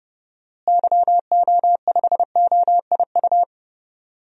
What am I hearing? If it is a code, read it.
Morse code, heard as a keyed tone, about 29 wpm, 700 Hz. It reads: YO5OIU
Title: DXCC-100 Call Signs - 6 Characters - Repeated 3x - 30wpm